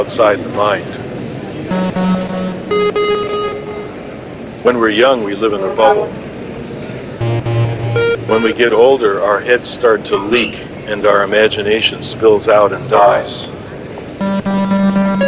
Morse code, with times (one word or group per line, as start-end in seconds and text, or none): none